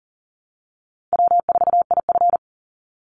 Morse code, heard as a keyed tone, about 40 wpm, 700 Hz. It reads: W4IF
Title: Call Signs Repeated 2x - 40wpm